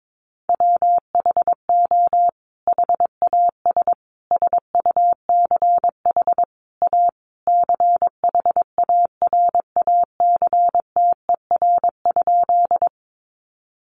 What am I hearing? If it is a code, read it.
Morse code, heard as a keyed tone, about 22 wpm, 700 Hz. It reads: WHO HAS SUCH A CHARACTER?
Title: Sentences from Top 1200 Words 22wpm